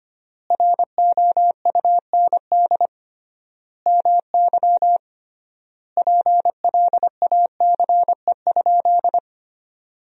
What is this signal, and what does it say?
Morse code, heard as a keyed tone, about 25 wpm, 700 Hz. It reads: ROUND MY PLACE?
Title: 3x Word-Spacing - Sentences from Top 200 Words - 25wpm